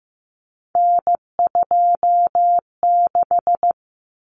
Morse code, heard as a keyed tone, about 15 wpm, 700 Hz. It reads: N26
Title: Common Cut Numbers - 3-Digit 15wpm